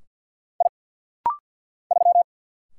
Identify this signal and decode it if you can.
Morse code, heard as a keyed tone, about 49 wpm, 690 Hz. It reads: I 3
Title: Single Letter-Number - Rapid-Fire 50wpm